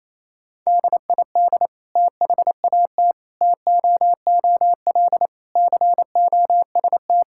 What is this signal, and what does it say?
Morse code, heard as a keyed tone, about 28 wpm, 700 Hz. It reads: DID THAT TOOL COST
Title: Sentences from Top 1000 Words - Review Part 6 - 28wpm